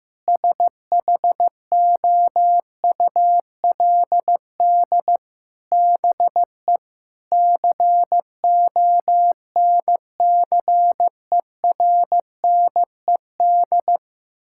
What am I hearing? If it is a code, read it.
Morse code, heard as a keyed tone, about 15 wpm, 700 Hz. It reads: SHOULD BE CONCERNED
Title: Sentences from Top 1000 Words - Review Part 6 - 15wpm